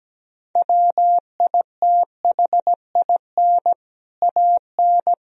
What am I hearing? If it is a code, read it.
Morse code, heard as a keyed tone, about 17 wpm, 700 Hz. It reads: WITHIN AN
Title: Sentences from Top 1000 Words - Review Part 4 - 17wpm